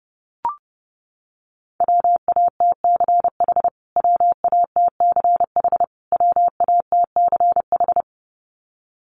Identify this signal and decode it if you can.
Morse code, heard as a keyed tone, about 30 wpm, 700 Hz. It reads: WATCH WATCH WATCH
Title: Top 500 Words Repeated 30wpm